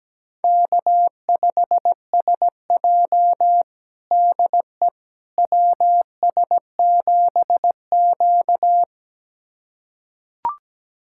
Morse code, heard as a keyed tone, about 17 wpm, 700 Hz. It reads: K5SJ DE WS7Q
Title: USA Callsign DE Callsign - 4 Characters - 17wpm